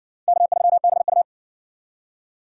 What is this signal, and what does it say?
Morse code, heard as a keyed tone, about 60 wpm, 700 Hz. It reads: K3BU